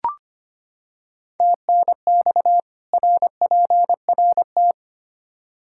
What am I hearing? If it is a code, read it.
Morse code, heard as a keyed tone, about 25 wpm, 700 Hz. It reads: TNX RPRT
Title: Practice QSO - Exchange 2 - Info - 25wpm